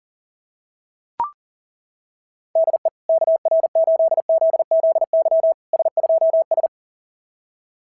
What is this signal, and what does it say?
Morse code, heard as a keyed tone, about 40 wpm, 690 Hz. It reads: DE KR8ZZY S2S
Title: SOTA QSO Practice 40wpm